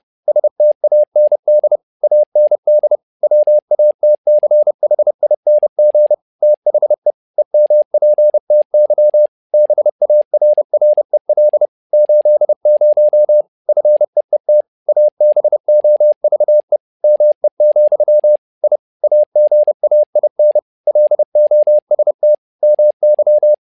Morse code, heard as a keyed tone, about 30 wpm, 600 Hz. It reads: STAND AND WATCHING THE EMPTY BARREL 80 FEET ABOVE ME, I AGAIN LOST MY
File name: learning center / tower trouble / tower trouble 30wpm